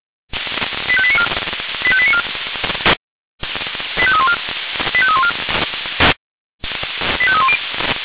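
Five-tone selective call (ZVEI), heard as quick runs of five tones, tones of about 70 ms. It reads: 85894 85894 85314 85314 85310
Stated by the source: ZVEI five-tone call received on SDR.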